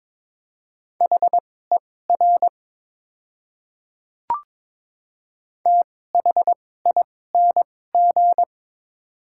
Morse code, heard as a keyed tone, about 22 wpm, spaced as for 16 wpm, 700 Hz. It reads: HER THING
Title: 2x Character Spacing - Top 100 Words - 22wpm